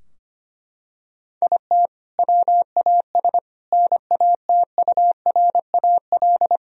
Morse code, heard as a keyed tone, about 25 wpm, 700 Hz. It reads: IT WAS NATURAL
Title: Sentences from Top 1000 Words - Review Part 4 - 25wpm